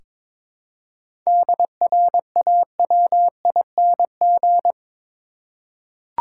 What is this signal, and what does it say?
Morse code, heard as a keyed tone, about 22 wpm, 700 Hz. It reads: DRAWING
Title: Seven Letter Words - 22wpm